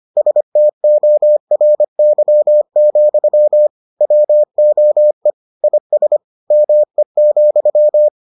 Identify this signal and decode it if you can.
Morse code, heard as a keyed tone, about 25 wpm, 600 Hz. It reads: STORY, WOE IS ME,